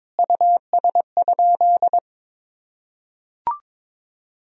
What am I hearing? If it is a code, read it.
Morse code, heard as a keyed tone, about 22 wpm, 700 Hz. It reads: US?
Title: Sentences from Top 1000 Words - Review Part 4 - 22wpm